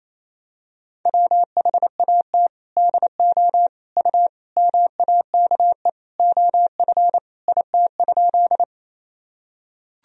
Morse code, heard as a keyed tone, about 28 wpm, 700 Hz. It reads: WHAT DO U MAKE OF IT?
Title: Sentences from Top 100 Words and Abbreviations 28wpm